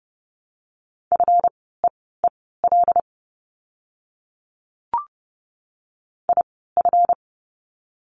Morse code, heard as a keyed tone, about 30 wpm, spaced as for 17 wpm, 700 Hz. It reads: FEEL IF
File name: course / 30:10wpm / 230 Lesson 37f - Letter x - Words 30wpm